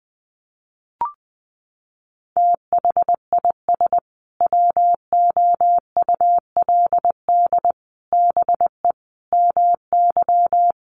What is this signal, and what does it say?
Morse code, heard as a keyed tone, about 20 wpm, 700 Hz. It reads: THIS WOULD BE MY
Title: Sentences from Top 1200 Words 20wpm